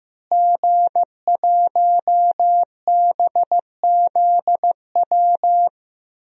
Morse code, heard as a keyed tone, about 15 wpm, 700 Hz. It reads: G1BZW
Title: DXCC-100 Call Signs - 5 Characters - Repeated 3x - 15wpm